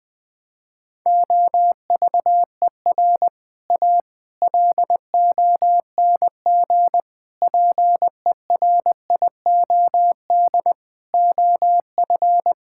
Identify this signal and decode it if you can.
Morse code, heard as a keyed tone, about 20 wpm, 700 Hz. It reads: OVER A LONG PERIOD OF
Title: Sentences with 6-Word Phrases - 20wpm